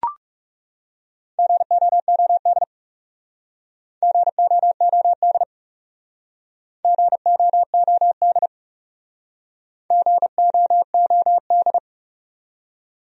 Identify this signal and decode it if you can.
Morse code, heard as a keyed tone, about 39 wpm, 700 Hz. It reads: GOOD GOOD GOOD GOOD